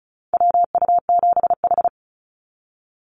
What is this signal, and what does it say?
Morse code, heard as a keyed tone, about 34 wpm, 700 Hz. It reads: WU7H
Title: NA SOTA Activator Callsigns 35wpm